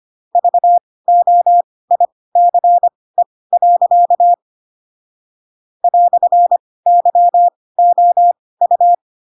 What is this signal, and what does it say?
Morse code, heard as a keyed tone, about 25 wpm, spaced as for 18 wpm, 700 Hz. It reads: VOICE. "YOU
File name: Pinocchio / Pinocchio ch2 12 25 ik1hge